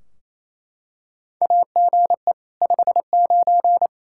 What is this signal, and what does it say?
Morse code, heard as a keyed tone, about 28 wpm, 700 Hz. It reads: AGE 59
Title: Practice QSO - Exchange 3 - More Info - 28wpm